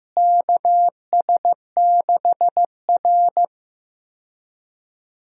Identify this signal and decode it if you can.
Morse code, heard as a keyed tone, about 15 wpm, 700 Hz. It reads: KS6R
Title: USA Call Signs - 4 Characters - Repeated 2x - 15wpm